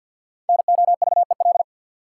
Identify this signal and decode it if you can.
Morse code, heard as a keyed tone, about 47 wpm, 700 Hz. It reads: NOVEL
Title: Sentences from Top 1300 Words - New Word Drill Alt - 50wpm